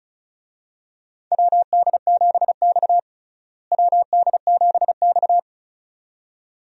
Morse code, heard as a keyed tone, about 36 wpm, 700 Hz. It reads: WD7X WD7X
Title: USA Call Signs - 4 Characters - Repeated 2x - 35wpm